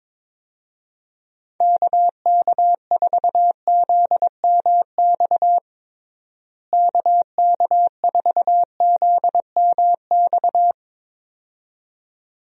Morse code, heard as a keyed tone, about 22 wpm, 700 Hz. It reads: KK4ZMX KK4ZMX
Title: USA Call Signs - 6 Characters - Repeated 2x - 22wpm